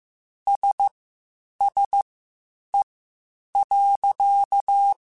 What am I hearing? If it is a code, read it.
Morse code, heard as a keyed tone, about 15 wpm, spaced as for 8 wpm, 800 Hz. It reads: SSE.
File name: Spaced Characters / Spaced Characters-mp3s / Weather 4 14 sp @13